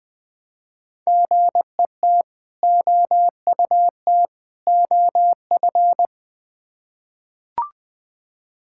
Morse code, heard as a keyed tone, about 21 wpm, 700 Hz. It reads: GET OUT OF
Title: Build A Phrase - 3-Word Phrases - 20wpm